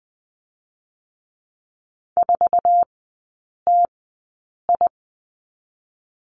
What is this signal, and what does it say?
Morse code, heard as a keyed tone, about 20 wpm, spaced as for 8 wpm, 700 Hz. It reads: 4TI
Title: 121 Lesson 20e - Number 3 - Three Characters 20wpm